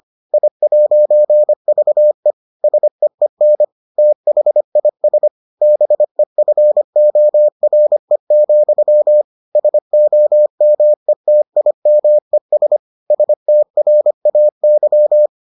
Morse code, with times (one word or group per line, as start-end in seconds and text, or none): 0.3-2.3 I'VE
2.6-3.6 SEEN
4.0-5.3 THIS
5.6-9.2 BEFORE,
9.6-12.8 SOMETIMES
13.1-15.3 STRAY